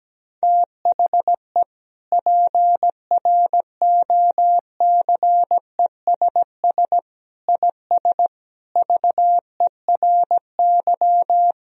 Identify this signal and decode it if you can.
Morse code, heard as a keyed tone, about 17 wpm, 700 Hz. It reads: THE PROCESS IS VERY
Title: Sentences from Top 1000 Words - Review Part 3 - 17wpm